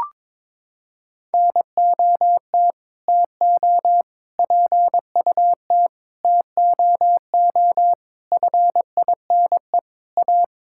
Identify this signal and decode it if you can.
Morse code, heard as a keyed tone, about 22 wpm, 700 Hz. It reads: NOT TO PUT TOO FINE A
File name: Sets of 9 Words - 22wpm